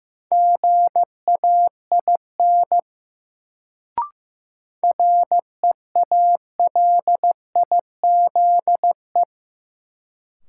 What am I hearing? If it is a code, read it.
Morse code, heard as a keyed tone, about 15 wpm, 700 Hz. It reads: GAIN REALIZE
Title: Sentences from Top 1000 Words - New Word Drill - 15wpm